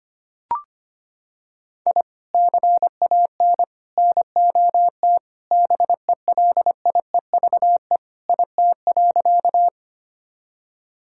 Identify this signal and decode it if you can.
Morse code, heard as a keyed tone, about 25 wpm, 700 Hz. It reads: I CAN NOT BELIEVE IT.
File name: practice / 25wpm / Sentences with 4-Word Phrases - 25wpm